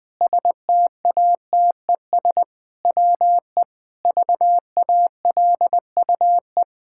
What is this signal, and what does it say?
Morse code, heard as a keyed tone, about 20 wpm, 700 Hz. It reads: STATES WE VALUE
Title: Sentences with 5-Word Phrases - Spoken Beginning - 20wpm